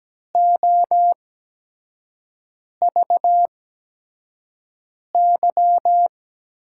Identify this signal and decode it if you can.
Morse code, heard as a keyed tone, about 17 wpm, spaced as for 5 wpm, 700 Hz. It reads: OVY
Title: USA Call Signs - 5 Characters - 8x Character Spacing - 17wpm